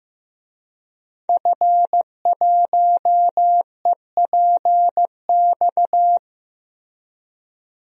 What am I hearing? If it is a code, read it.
Morse code, heard as a keyed tone, about 16 wpm, 700 Hz. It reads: F1EPX
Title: DXCC-100 Call Signs - 5 Characters - Repeated 2x - 15wpm